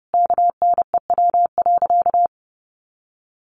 Morse code, heard as a keyed tone, about 28 wpm, 700 Hz. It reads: KNEW.
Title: Sentences from Top 400 Words 30wpm